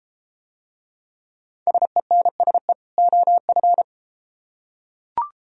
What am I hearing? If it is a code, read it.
Morse code, heard as a keyed tone, about 33 wpm, 700 Hz. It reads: SENSE OF